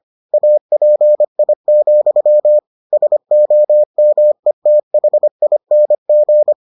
Morse code, heard as a keyed tone, about 25 wpm, 600 Hz. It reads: API, SOMETHING